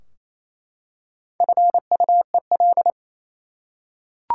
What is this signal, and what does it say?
Morse code, heard as a keyed tone, about 28 wpm, 700 Hz. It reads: FUEL
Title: Four Letter Words - 28wpm